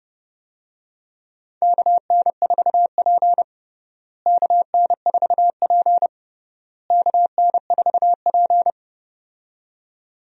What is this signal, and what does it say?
Morse code, heard as a keyed tone, about 32 wpm, 700 Hz. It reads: KN4P KN4P KN4P
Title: USA Call Signs - 4 Characters - Repeated 3x - 30wpm